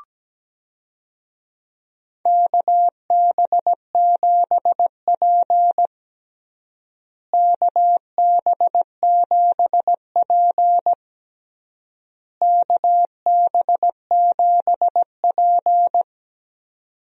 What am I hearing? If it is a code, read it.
Morse code, heard as a keyed tone, about 17 wpm, 700 Hz. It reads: KB7P KB7P KB7P
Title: USA Call Signs - 4 Characters - Repeated 3x - 17wpm